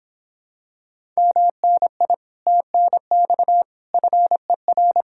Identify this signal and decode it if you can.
Morse code, heard as a keyed tone, about 26 wpm, 700 Hz. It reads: MNI TNX FER